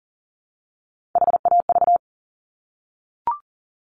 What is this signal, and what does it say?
Morse code, heard as a keyed tone, about 38 wpm, 700 Hz. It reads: HAV
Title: Top Three Letter Combinations in Words 40wpm